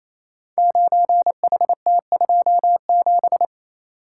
Morse code, heard as a keyed tone, about 27 wpm, 700 Hz. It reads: 9HT27
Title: Code Groups - 5 Letter:Numbers - 28wpm